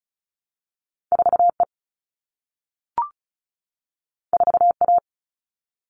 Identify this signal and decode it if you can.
Morse code, heard as a keyed tone, about 36 wpm, 700 Hz. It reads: 4E 4A
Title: All Cut Numbers - 2-Digit 35wpm